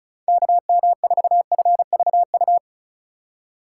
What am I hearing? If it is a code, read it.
Morse code, heard as a keyed tone, about 34 wpm, 700 Hz. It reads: KM4FVU
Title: USA Call Signs - 6 Characters - 2x Character Spacing - 35wpm